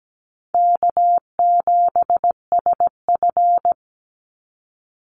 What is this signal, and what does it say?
Morse code, heard as a keyed tone, about 17 wpm, 700 Hz. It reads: K7SF
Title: Easy Call Signs - 17wpm